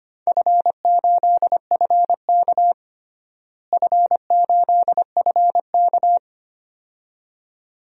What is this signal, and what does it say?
Morse code, heard as a keyed tone, about 25 wpm, 700 Hz. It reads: F8FK F8FK
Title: DXCC-100 Call Signs - 4 Characters - Repeated 2x - 25wpm